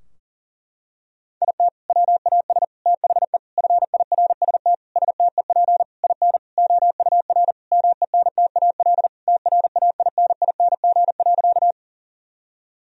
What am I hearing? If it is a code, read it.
Morse code, heard as a keyed tone, about 40 wpm, 700 Hz. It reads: IT WAS THE FIRST STEP IN OUR MENTAL TRAINING.
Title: Sentences from Top 1000 Words - Review Part 4 - 40wpm